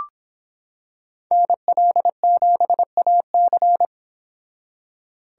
Morse code, heard as a keyed tone, about 25 wpm, 700 Hz. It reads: NL7AC